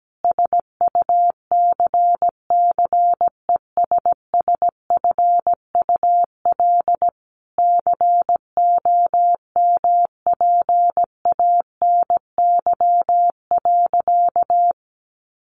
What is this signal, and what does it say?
Morse code, heard as a keyed tone, about 17 wpm, 700 Hz. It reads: SUCCESSFUL COMPANY.
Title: Sentences from Top 1100 Words 17wpm